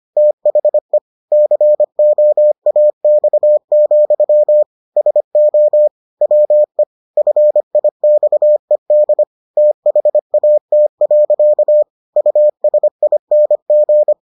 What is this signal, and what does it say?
Morse code, heard as a keyed tone, about 25 wpm, 600 Hz. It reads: THE COAX, SO WE FIXED THAT. USING